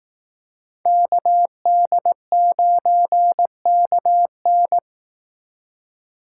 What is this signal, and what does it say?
Morse code, heard as a keyed tone, about 18 wpm, 700 Hz. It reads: KD9KN
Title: Speed Racing - USA 2x2 Call Signs 15wpm